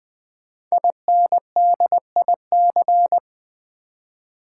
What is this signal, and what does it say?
Morse code, heard as a keyed tone, about 20 wpm, 700 Hz. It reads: INDIC
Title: Build A Word - 5 Letters At A Time - 10-Letter Words - 20wpm